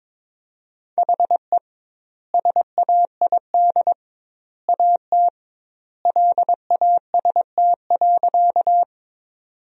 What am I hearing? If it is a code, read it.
Morse code, heard as a keyed tone, about 22 wpm, 700 Hz. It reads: HE SAID AT LAST.
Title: 2x Word-Spacing - Sentences from Top 300 Words - 22wpm